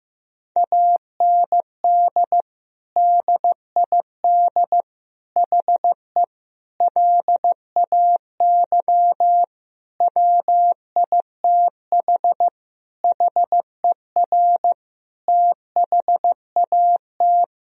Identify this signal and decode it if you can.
Morse code, heard as a keyed tone, about 15 wpm, 700 Hz. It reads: AND DID HE LAY WITH HER THAT
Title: Sentences from Top 1000 Words - Review Part 2 - 15wpm